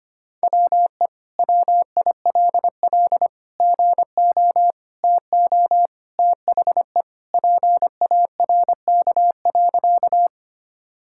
Morse code, heard as a keyed tone, about 25 wpm, 700 Hz. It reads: WE WILL GO TO THE PARK.
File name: practice / 25wpm / Sentences with 2-Word Phrases - 25wpm